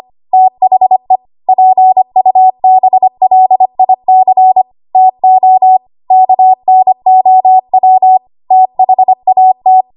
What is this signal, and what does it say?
Morse code, heard as a keyed tone, about 25 wpm, 750 Hz. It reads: THE PUBLIC TO KNOW THAT
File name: Morse / 25WPM